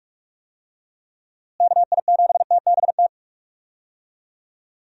Morse code, heard as a keyed tone, about 46 wpm, 700 Hz. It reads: KI7TBT
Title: USA Call Signs - 6 Characters - Repeated 3x - 45wpm